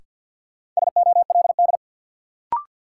Morse code, heard as a keyed tone, about 48 wpm, 700 Hz. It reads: SOLD
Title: Sentences from Top 1400 Words - New Word Drill Alt - 50wpm